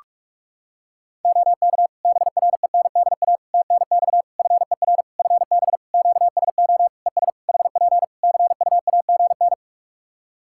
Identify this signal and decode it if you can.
Morse code, heard as a keyed tone, about 46 wpm, 700 Hz. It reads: OK BRENDA TNX FER FB QSO ES HP CUAGN
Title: Practice QSO - Exchange 4 - Ending - 45wpm